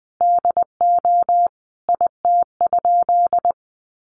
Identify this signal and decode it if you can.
Morse code, heard as a keyed tone, about 20 wpm, 700 Hz. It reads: DO IT?
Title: Sentences from Top 1200 Words 20wpm